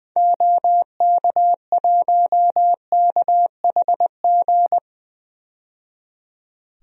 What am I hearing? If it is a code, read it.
Morse code, heard as a keyed tone, about 21 wpm, 700 Hz. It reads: OK1KHG
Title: DXCC-100 Call Signs - 6 Characters - 20wpm